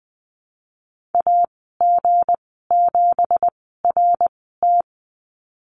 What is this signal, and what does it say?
Morse code, heard as a keyed tone, about 20 wpm, spaced as for 14 wpm, 700 Hz. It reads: AG7RT
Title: 179 Lesson 29g - Slash - Call Signs 20wpm